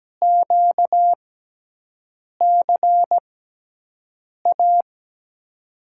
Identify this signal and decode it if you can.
Morse code, heard as a keyed tone, about 17 wpm, spaced as for 6 wpm, 700 Hz. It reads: QCA
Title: USA Call Signs - 6 Characters - 6x Character Spacing - 17wpm